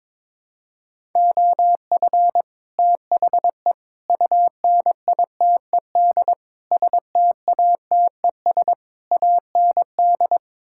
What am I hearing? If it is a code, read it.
Morse code, heard as a keyed tone, about 22 wpm, 700 Hz. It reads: OF THE UNITED STATES AND
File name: practice / 22wpm / Build A Phrase - 5-Word Phrases - 22wpm